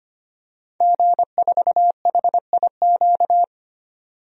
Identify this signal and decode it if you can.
Morse code, heard as a keyed tone, about 25 wpm, 700 Hz. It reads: G4HIQ
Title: DXCC-100 Call Signs - 5 Characters - 25wpm